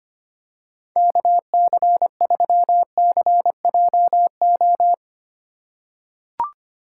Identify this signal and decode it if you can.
Morse code, heard as a keyed tone, about 25 wpm, 700 Hz. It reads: KC3CJO
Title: USA Call Signs - 6 Characters - 4x Character Spacing - 25wpm